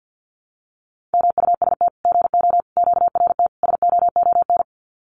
Morse code, heard as a keyed tone, about 50 wpm, 700 Hz. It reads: MUST GO QRT SOON
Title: Common QSO Phrases 50wpm